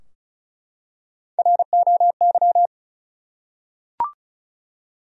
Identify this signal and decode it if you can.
Morse code, heard as a keyed tone, about 34 wpm, 700 Hz. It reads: ROY
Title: USA Common Names - Max 5 Letters - 35wpm